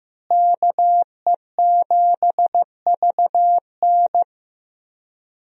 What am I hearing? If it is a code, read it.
Morse code, heard as a keyed tone, about 15 wpm, 700 Hz. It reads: KE7VN
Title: USA Call Signs - 5 Characters - Repeated 2x - 15wpm